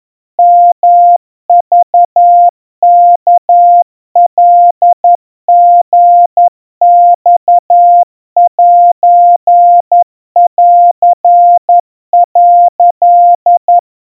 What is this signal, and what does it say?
Morse code (Japanese Wabun code), heard as a keyed tone, about 11 wpm, 700 Hz. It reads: ヨクワカリマセン」
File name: kawa / Morse sample / kamyu 40 cpm 700 Hz Crispy